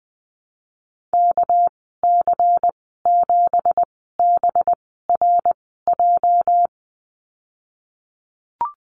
Morse code, heard as a keyed tone, about 20 wpm, spaced as for 15 wpm, 700 Hz. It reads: KC7BRJ
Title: 219 Lesson 35g - Number 0 - Call Signs 20wpm